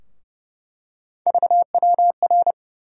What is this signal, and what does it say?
Morse code, heard as a keyed tone, about 30 wpm, 700 Hz. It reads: VWR
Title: USA Call Sign Postfixes - 3 Letters 30wpm